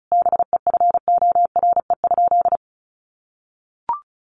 Morse code, heard as a keyed tone, about 34 wpm, 700 Hz. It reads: BEFORE?
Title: Sentences from Top 1200 Words 35wpm